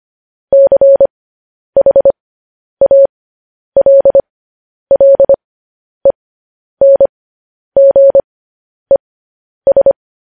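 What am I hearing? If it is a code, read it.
Morse code, heard as a keyed tone, about 25 wpm, spaced as for 10 wpm, 550 Hz. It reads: CHALLENGES
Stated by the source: Morse code news headlines broadcast.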